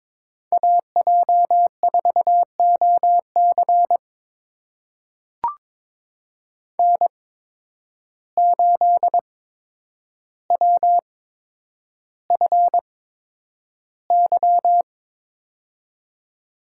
Morse code, heard as a keyed tone, about 22 wpm, spaced as for 9 wpm, 700 Hz. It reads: AJ4OC N8WFY
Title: USA Call Signs - 5 Characters - 8x Character Spacing - 22wpm